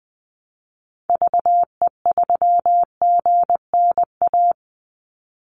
Morse code, heard as a keyed tone, about 20 wpm, 700 Hz. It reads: VE3GNA
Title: Call Signs - 20wpm